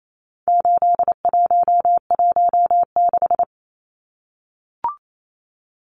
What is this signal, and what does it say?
Morse code, heard as a keyed tone, about 28 wpm, 700 Hz. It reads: 8116
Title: Numbers - 4-Digit 28wpm